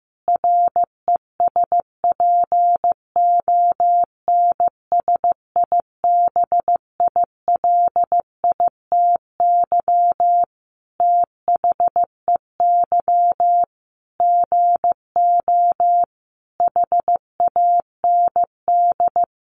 Morse code, heard as a keyed tone, about 15 wpm, 700 Hz. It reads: RESPONSIBILITY THEY GO HAND
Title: Sentences from Top 1200 Words 15wpm